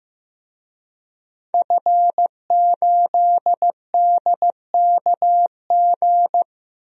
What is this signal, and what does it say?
Morse code, heard as a keyed tone, about 15 wpm, 700 Hz. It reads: F8DKG
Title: DXCC-100 Call Signs - 5 Characters - Repeated 2x - 15wpm